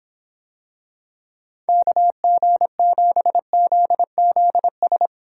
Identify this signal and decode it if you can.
Morse code, heard as a keyed tone, about 26 wpm, 700 Hz. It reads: KG7ZZS